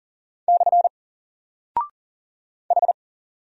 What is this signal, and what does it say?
Morse code, heard as a keyed tone, about 40 wpm, 700 Hz. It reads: / H
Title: Single Character (Letter:Number:Punctuation:Prosign) Alt 40wpm